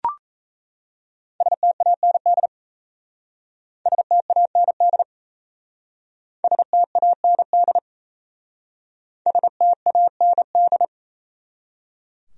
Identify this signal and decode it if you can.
Morse code, heard as a keyed tone, about 33 wpm, 700 Hz. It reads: STAND STAND STAND STAND
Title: Speed-Racing Top 300 Words 28wpm